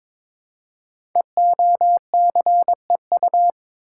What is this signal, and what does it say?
Morse code, heard as a keyed tone, about 22 wpm, 700 Hz. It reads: EOCEU
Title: Code Groups - 5 Letter:Numbers - 22wpm